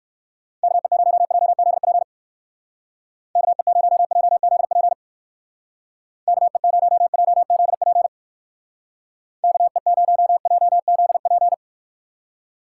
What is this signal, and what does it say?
Morse code, heard as a keyed tone, about 57 wpm, 700 Hz. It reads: KE0JZP KE0JZP KE0JZP KE0JZP